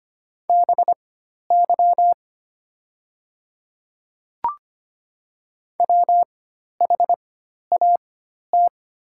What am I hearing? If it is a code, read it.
Morse code, heard as a keyed tone, about 25 wpm, spaced as for 12 wpm, 690 Hz. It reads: BY WHAT
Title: 4x Character Spacing - Top 100 Words - 25wpm